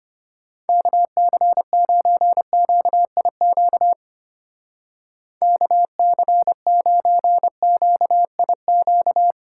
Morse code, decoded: KC9QIQ KC9QIQ

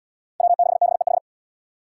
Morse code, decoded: W6BEH

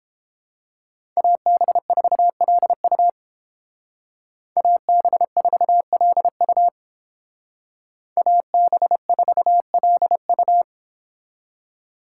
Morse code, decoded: AB4LU AB4LU AB4LU

